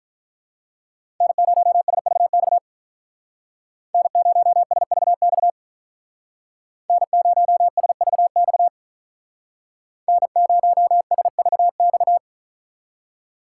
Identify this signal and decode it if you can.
Morse code, heard as a keyed tone, about 46 wpm, 700 Hz. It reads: N0SVX N0SVX N0SVX N0SVX